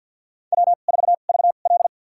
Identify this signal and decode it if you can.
Morse code, heard as a keyed tone, about 53 wpm, spaced as for 37 wpm, 700 Hz. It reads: W4VL
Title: USA Call Signs - 4 Characters - 2x Character Spacing - 50wpm